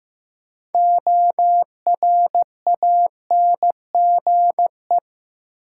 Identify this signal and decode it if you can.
Morse code, heard as a keyed tone, about 15 wpm, 700 Hz. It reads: ORANGE
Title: Six Letter Words - 15wpm